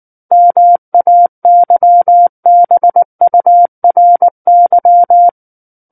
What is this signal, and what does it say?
Morse code, heard as a keyed tone, about 19 wpm, 700 Hz. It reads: MAYBURY
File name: Book One / 10-Chapter 10 - 19 WPM